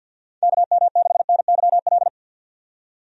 Morse code, heard as a keyed tone, about 47 wpm, 700 Hz. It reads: KM6NYL
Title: USA Call Signs - 6 Characters - 50wpm